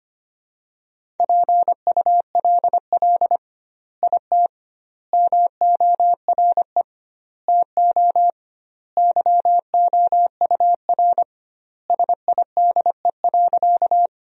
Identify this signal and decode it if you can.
Morse code, heard as a keyed tone, about 25 wpm, 700 Hz. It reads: PULL IT MORE TO YOUR SIDE.
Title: 2x Word-Spacing - Sentences from Top 400 Words - 25wpm